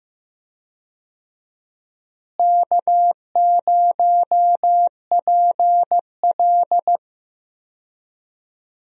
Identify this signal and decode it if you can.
Morse code, heard as a keyed tone, about 14 wpm, 700 Hz. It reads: K0PL
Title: USA Call Signs - 4 Characters - Repeated 3x - 15wpm